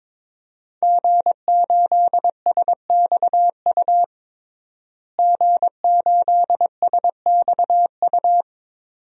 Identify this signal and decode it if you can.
Morse code, heard as a keyed tone, about 22 wpm, 700 Hz. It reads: G8SXU G8SXU
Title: DXCC-100 Call Signs - 5 Characters - Repeated 2x - 22wpm